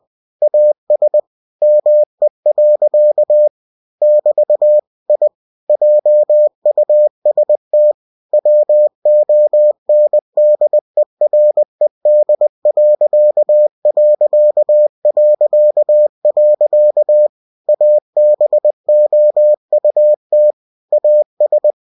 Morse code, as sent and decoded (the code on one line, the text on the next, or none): .- ... / -- . .-.-.- / -...- / .. / .--- ..- ... - / .-- --- -. -.. .-. -.. .-.-.- .-.-.- .-.-.- .-.-.- / .- -... --- ..- - / .- ...
AS ME. <BT> I JUST WONDERED.... ABOUT AS